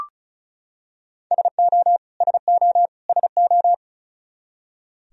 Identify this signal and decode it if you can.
Morse code, heard as a keyed tone, about 34 wpm, 700 Hz. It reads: SO SO SO
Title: Top 100 Words Repeated 35wpm